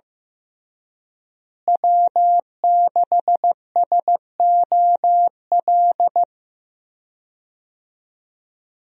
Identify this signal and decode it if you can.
Morse code, heard as a keyed tone, about 15 wpm, 700 Hz. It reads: W6SOL
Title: USA Call Signs - 5 Characters - Repeated 3x - 15wpm